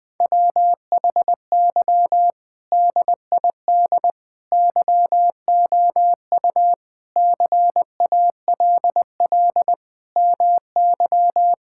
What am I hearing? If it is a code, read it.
Morse code, heard as a keyed tone, about 20 wpm, 700 Hz. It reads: WHY DID YOU CALL MY
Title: Sentences from Top 1000 Words - Review Part 6 - 20wpm